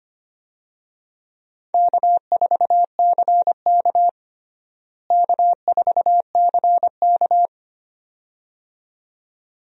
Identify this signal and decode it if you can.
Morse code, heard as a keyed tone, about 25 wpm, 700 Hz. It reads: K4CK K4CK
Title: USA Call Signs - 4 Characters - Repeated 2x - 25wpm